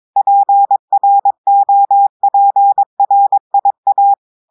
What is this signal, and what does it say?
Morse code, heard as a keyed tone, about 22 wpm, 800 Hz. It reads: PROPRIA
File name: audios / portugais / portugais texte en clair 1 vitesse 22 mots minutes